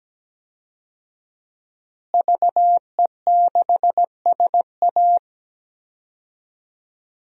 Morse code, heard as a keyed tone, about 17 wpm, 700 Hz. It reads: VE6SA
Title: DXCC-100 Call Signs - 5 Characters - Repeated 3x - 17wpm